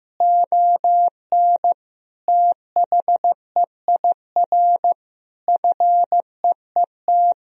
Morse code, heard as a keyed tone, about 15 wpm, 700 Hz. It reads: ON THEIR FEET
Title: Sentences from Top 1000 Words - Review Part 1 - 15wpm